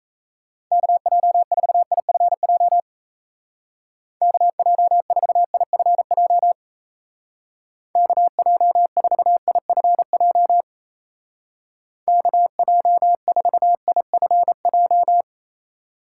Morse code, heard as a keyed tone, about 34 wpm, 700 Hz. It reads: KJ4IFJ KJ4IFJ KJ4IFJ KJ4IFJ
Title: Speed Racing - USA 2x3 Call Signs 28wpm